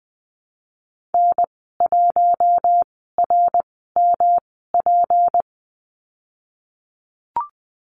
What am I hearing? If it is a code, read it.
Morse code, heard as a keyed tone, about 20 wpm, spaced as for 15 wpm, 700 Hz. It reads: N1RMP
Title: 186 Lesson 30g - Letter b - Call Signs 20wpm